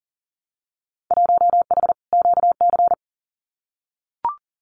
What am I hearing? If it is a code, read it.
Morse code, heard as a keyed tone, about 40 wpm, 700 Hz. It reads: 1H QC
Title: Winter Field Day Exchanges 40wpm